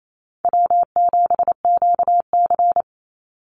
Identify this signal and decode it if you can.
Morse code, heard as a keyed tone, about 28 wpm, 700 Hz. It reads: W7QC
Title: Easy Call Signs - 28wpm